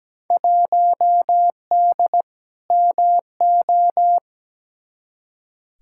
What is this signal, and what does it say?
Morse code, heard as a keyed tone, about 17 wpm, 700 Hz. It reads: 1D MO
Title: Field Day Exchanges - 17wpm